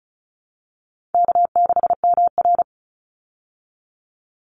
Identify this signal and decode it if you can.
Morse code, heard as a keyed tone, about 34 wpm, 700 Hz. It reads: K6MR
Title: Easy Call Signs Repeated 2x - 35wpm